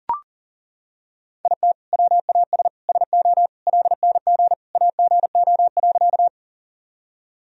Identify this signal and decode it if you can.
Morse code, heard as a keyed tone, about 40 wpm, 700 Hz. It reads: IT WAS SO LNG AGO.
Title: Sentences from Top 500 Words and Abbreviations 40wpm